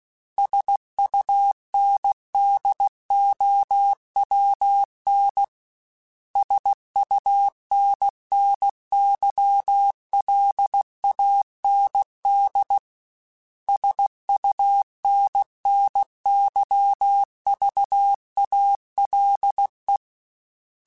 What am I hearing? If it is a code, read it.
Morse code, heard as a keyed tone, about 16 wpm, 800 Hz. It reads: SUNDOWN SUNNYLAND SUNNYVALE
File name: fromSMs / FromSMs mp3s / Cities@15